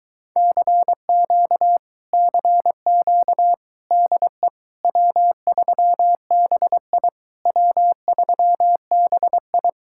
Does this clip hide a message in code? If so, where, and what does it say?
Yes: Morse code from 0.4 to 9.7 s, heard as a keyed tone, about 22 wpm, 700 Hz. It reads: CQ CQ DE W3BI W3BI